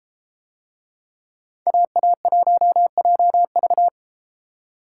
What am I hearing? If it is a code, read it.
Morse code, heard as a keyed tone, about 33 wpm, 700 Hz. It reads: AA1JV